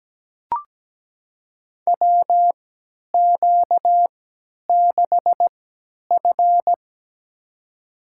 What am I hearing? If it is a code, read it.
Morse code, heard as a keyed tone, about 17 wpm, spaced as for 10 wpm, 700 Hz. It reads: WQ6F
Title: USA Call Signs - 4 Characters - 3x Character Spacing - 17wpm